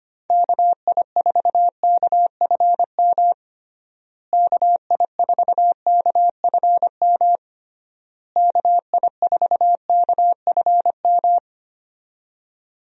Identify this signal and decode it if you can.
Morse code, heard as a keyed tone, about 25 wpm, 700 Hz. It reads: KI4KFM KI4KFM KI4KFM
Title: USA Call Signs - 6 Characters - Repeated 3x - 25wpm